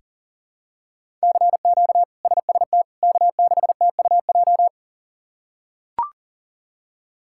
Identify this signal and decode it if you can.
Morse code, heard as a keyed tone, about 40 wpm, 700 Hz. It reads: CQ SST K6TUJ
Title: K1USN SST QSO Practice - 40wpm